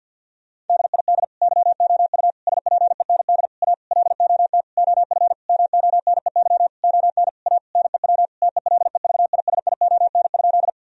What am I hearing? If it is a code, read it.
Morse code, heard as a keyed tone, about 50 wpm, 700 Hz. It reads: DID YOU SPEND A LOT OF MONEY ON A NEW TELEVISION?
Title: Sentences from Top 1000 Words - Review Part 4 - 50wpm